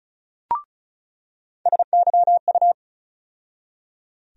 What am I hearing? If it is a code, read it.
Morse code, heard as a keyed tone, about 34 wpm, 700 Hz. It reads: SYU
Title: USA Call Sign Postfixes - 3 Letters 35wpm